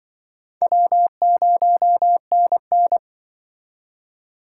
Morse code, heard as a keyed tone, about 25 wpm, 700 Hz. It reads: W0NN
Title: Speed Racing - USA 1x2 Call Signs 20wpm